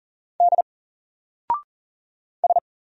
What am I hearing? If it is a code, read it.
Morse code, heard as a keyed tone, about 40 wpm, 700 Hz. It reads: D S
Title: Single Character (Letter:Number:Punctuation:Prosign) 40wpm